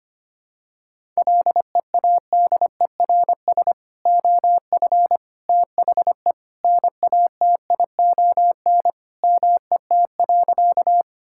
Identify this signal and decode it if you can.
Morse code, heard as a keyed tone, about 25 wpm, 700 Hz. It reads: LEADERS OF THE NATION MET.
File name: practice / 25wpm / Sentences with 3-Word Phrases - 25wpm